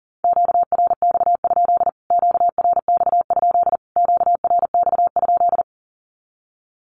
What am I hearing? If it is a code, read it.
Morse code, heard as a keyed tone, about 40 wpm, 700 Hz. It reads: QRX? QRX? QRX?
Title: Q-Codes Repeated 3x 40wpm